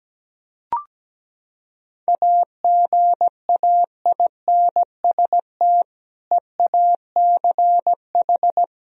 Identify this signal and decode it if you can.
Morse code, heard as a keyed tone, about 17 wpm, 700 Hz. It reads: AGAINST EACH
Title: Sentences from Top 1000 Words - Review Part 1 - 17wpm